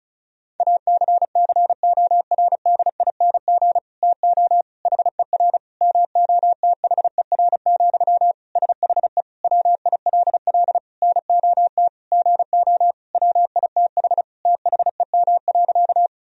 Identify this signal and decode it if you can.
Morse code, heard as a keyed tone, about 36 wpm, 700 Hz. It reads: ACCORDING TO HER MOTHER, SHE WILL NOT GO WITH THEM.
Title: Sentences from Top 1000 Words - Review Part 4 - 35wpm